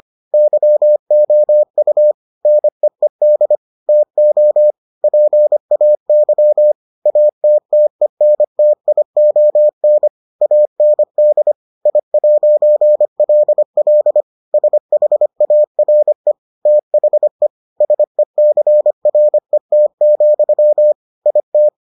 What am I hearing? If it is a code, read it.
Morse code, heard as a keyed tone, about 25 wpm, 600 Hz. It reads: YOU NEED TO PAY ATTENTION AND I'LL SHARE THE SECRET, IT